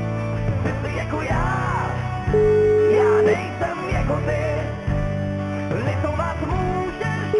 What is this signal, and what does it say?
Ringback tone (Europe, 425 Hz), with noise.